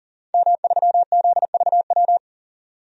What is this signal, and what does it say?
Morse code, heard as a keyed tone, about 40 wpm, 700 Hz. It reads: M3ZVW
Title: DXCC-100 Call Signs - 5 Characters - 40wpm